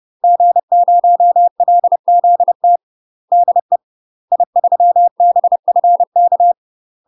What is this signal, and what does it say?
Morse code, heard as a keyed tone, about 30 wpm, 700 Hz. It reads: G0LZT DE I3BFK
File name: UK MASTER QSOs TRAINING / 30WPM / UK MASTER QSOs 0026